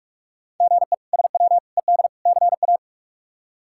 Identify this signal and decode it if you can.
Morse code, heard as a keyed tone, about 46 wpm, 700 Hz. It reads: GE SW ED CA
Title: K1USN SST QSO Practice - 45wpm